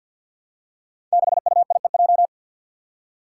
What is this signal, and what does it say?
Morse code, heard as a keyed tone, about 46 wpm, 700 Hz. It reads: 6UIEJ
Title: Five Challenging Letter-Number Combinations 50wpm